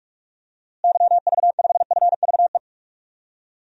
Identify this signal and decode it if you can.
Morse code, heard as a keyed tone, about 43 wpm, 700 Hz. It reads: YV5FVE
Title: DXCC-100 Call Signs - 6 Characters - Repeated 3x - 45wpm